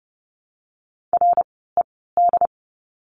Morse code, heard as a keyed tone, about 30 wpm, spaced as for 17 wpm, 700 Hz. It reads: RED
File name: Lesson 17f - Letter u - Words 30wpm